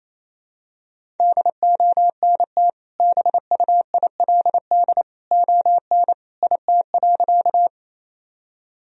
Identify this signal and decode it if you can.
Morse code, heard as a keyed tone, about 28 wpm, 700 Hz. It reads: DONT BUILD ON IT.
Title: Sentences from Top 1000 Words - Review Part 1 - 28wpm